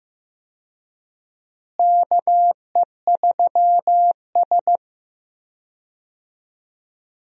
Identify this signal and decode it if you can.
Morse code, heard as a keyed tone, about 15 wpm, 700 Hz. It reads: KE3S
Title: USA Call Signs - 4 Characters - Repeated 2x - 15wpm